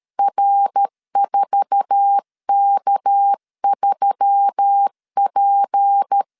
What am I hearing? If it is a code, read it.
Morse code, heard as a keyed tone, about 13 wpm, 800 Hz. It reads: R4K3P
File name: CW / DET / 1 9 1500C S B 12wpm 3819